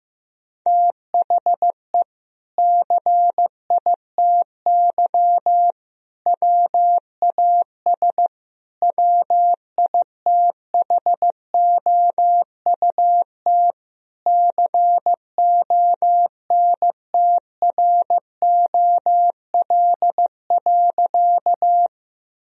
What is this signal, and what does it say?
Morse code, heard as a keyed tone, about 15 wpm, 700 Hz. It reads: THE CITY WAS WITHOUT CONTROL.
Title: Sentences from Top 1000 Words - Review Part 3 - 15wpm